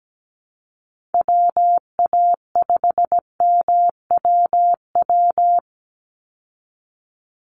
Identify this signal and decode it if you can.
Morse code, heard as a keyed tone, about 17 wpm, 700 Hz. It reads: WA5MWW
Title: USA Call Signs 17wpm